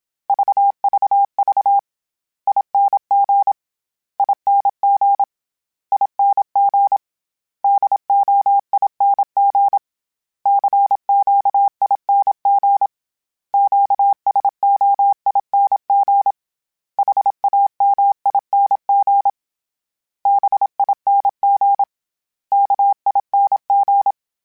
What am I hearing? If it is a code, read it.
Morse code, heard as a keyed tone, about 27 wpm, 800 Hz. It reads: VVV ING ING ING DOING CQING QSOING HAMING BING KING